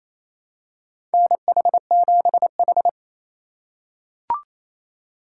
Morse code, heard as a keyed tone, about 28 wpm, 700 Hz. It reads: NH7H
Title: USA Call Signs - 4 Characters - 2x Character Spacing - 28wpm